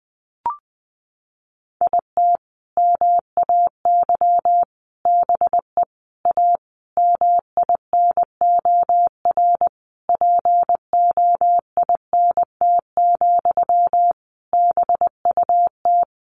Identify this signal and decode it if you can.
Morse code, heard as a keyed tone, about 20 wpm, 700 Hz. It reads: IT MAY BE A MINOR POINT, BUT
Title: Sentences from Top 1500 Words 20wpm